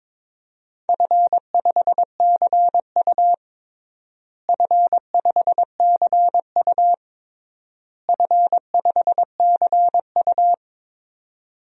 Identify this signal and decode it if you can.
Morse code, heard as a keyed tone, about 22 wpm, 700 Hz. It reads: F5CU F5CU F5CU
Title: DXCC-100 Call Signs - 4 Characters - Repeated 3x - 22wpm